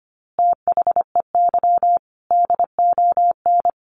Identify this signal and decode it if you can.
Morse code, heard as a keyed tone, about 25 wpm, 700 Hz. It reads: THEY DON
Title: Easy Phrases 25wpm